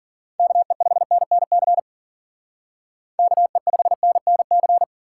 Morse code, heard as a keyed tone, about 52 wpm, 700 Hz. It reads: KE5NNC KE5NNC